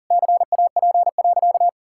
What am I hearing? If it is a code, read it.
Morse code, heard as a keyed tone, about 40 wpm, 700 Hz. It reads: CAP.
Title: Sentences from Top 1600 Words 40wpm